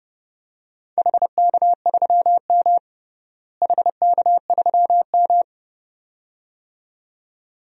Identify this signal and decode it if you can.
Morse code, heard as a keyed tone, about 32 wpm, 700 Hz. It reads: HK3M HK3M
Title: DXCC-100 Call Signs - 4 Characters - Repeated 2x - 30wpm